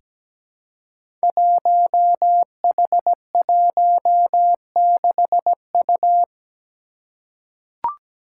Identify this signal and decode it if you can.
Morse code, heard as a keyed tone, about 17 wpm, 700 Hz. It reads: 1H16U
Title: Five Challenging Letter-Number Combinations 17wpm